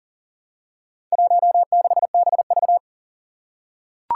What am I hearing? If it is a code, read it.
Morse code, heard as a keyed tone, about 37 wpm, 700 Hz. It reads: 16BV
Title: Four Challenging Letter-Number Combinations 40wpm